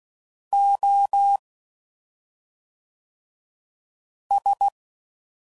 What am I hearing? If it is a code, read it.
Morse code, heard as a keyed tone, about 16 wpm, 800 Hz. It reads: O S
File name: Suffixes / Special ING's / SlowToFaster ING's